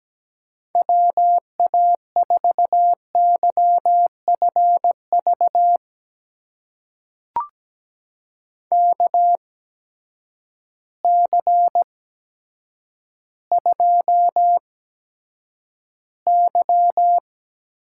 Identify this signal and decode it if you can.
Morse code, heard as a keyed tone, about 17 wpm, spaced as for 8 wpm, 700 Hz. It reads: WA4YFV KC2Y
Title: USA Call Signs - 8x Character Spacing - 17wpm